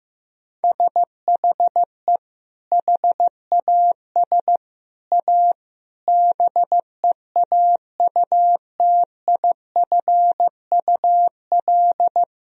Morse code, heard as a keyed tone, about 16 wpm, 700 Hz. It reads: SHE HAS A BEAUTIFUL
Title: Sentences from Top 1000 Words - Review Part 2 - 15wpm